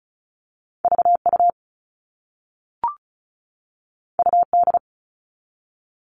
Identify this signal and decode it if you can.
Morse code, heard as a keyed tone, about 36 wpm, 700 Hz. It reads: VU UD
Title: All Cut Numbers - 2-Digit 35wpm